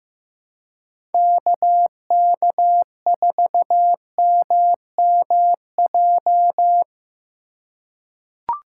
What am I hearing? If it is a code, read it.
Morse code, heard as a keyed tone, about 15 wpm, 700 Hz. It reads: KK4MMJ
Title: USA Call Signs - 6 Characters - 3x Character Spacing - 15wpm